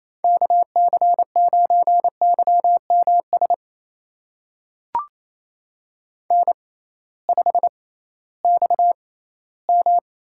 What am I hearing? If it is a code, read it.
Morse code, heard as a keyed tone, about 27 wpm, spaced as for 16 wpm, 700 Hz. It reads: KC9YMS N5XM
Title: USA Call Signs - 6x Character Spacing - 28wpm